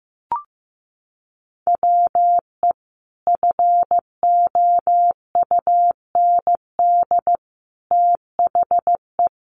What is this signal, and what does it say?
Morse code, heard as a keyed tone, about 15 wpm, 700 Hz. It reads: WE FOUND THE
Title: Sentences from Top 1300 Words 15wpm